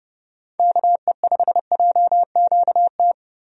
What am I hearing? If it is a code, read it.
Morse code, heard as a keyed tone, about 30 wpm, 700 Hz. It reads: KE5JQT
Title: USA Call Signs - 6 Characters - 30wpm